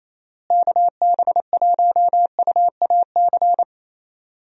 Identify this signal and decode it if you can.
Morse code, heard as a keyed tone, about 28 wpm, 700 Hz. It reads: KB1UAC
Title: USA Call Signs - 6 Characters - Repeated 3x - 28wpm